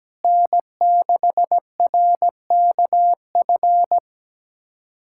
Morse code, heard as a keyed tone, about 17 wpm, 700 Hz. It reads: N6RKF
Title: USA Callsign DE Callsign - 5 Characters - 17wpm